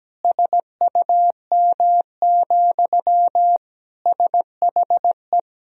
Morse code, heard as a keyed tone, about 17 wpm, 700 Hz. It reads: SUM, SHE
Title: Sentences from Top 200 Words and Abbreviations 17wpm